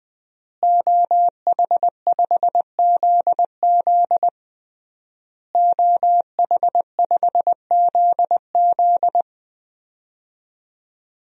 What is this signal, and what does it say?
Morse code, heard as a keyed tone, about 21 wpm, 700 Hz. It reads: OH5ZZ OH5ZZ
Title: DXCC-100 Call Signs - 5 Characters - Repeated 3x - 20wpm